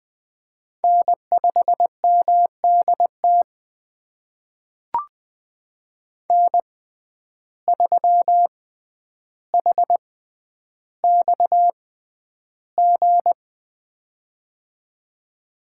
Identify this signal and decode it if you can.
Morse code, heard as a keyed tone, about 20 wpm, spaced as for 10 wpm, 700 Hz. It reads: N5MDT N3HXG
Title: USA Call Signs - 5 Characters - 6x Character Spacing - 20wpm